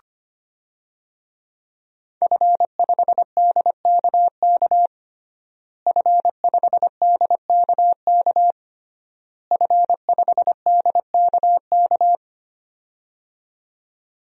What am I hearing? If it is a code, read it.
Morse code, heard as a keyed tone, about 25 wpm, 700 Hz. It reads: F5DKK F5DKK F5DKK
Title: DXCC-100 Call Signs - 5 Characters - Repeated 3x - 25wpm